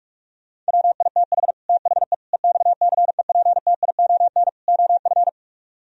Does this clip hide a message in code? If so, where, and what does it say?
Yes: Morse code from 0.7 to 5.3 s, heard as a keyed tone, about 58 wpm, 700 Hz. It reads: WITH THE EXCEPTION OF